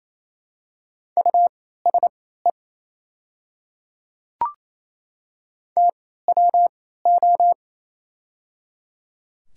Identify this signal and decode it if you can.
Morse code, heard as a keyed tone, about 28 wpm, spaced as for 16 wpm, 700 Hz. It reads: USE TWO
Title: 3x Character Spacing - Top 100 Words - 28wpm